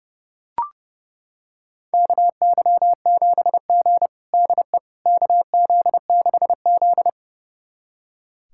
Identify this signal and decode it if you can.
Morse code, heard as a keyed tone, about 30 wpm, 700 Hz. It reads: KY7G DE KZ6Z
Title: USA Callsign DE Callsign - 4 Characters - 30wpm